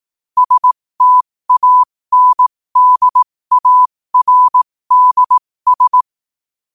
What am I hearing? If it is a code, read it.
Morse code, heard as a keyed tone, about 18 wpm, spaced as for 15 wpm, 1000 Hz. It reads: STANDARDS
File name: qotdfiles / 15wpm